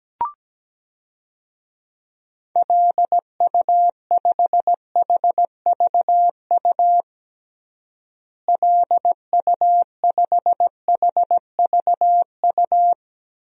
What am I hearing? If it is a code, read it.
Morse code, heard as a keyed tone, about 17 wpm, 700 Hz. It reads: LU5HVU LU5HVU
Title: DXCC-100 Call Signs - 6 Characters - Repeated 2x - 17wpm